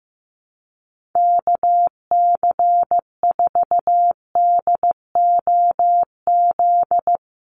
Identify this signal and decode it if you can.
Morse code, heard as a keyed tone, about 14 wpm, 700 Hz. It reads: KC4DOZ
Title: USA Call Signs Repeated 3x 15wpm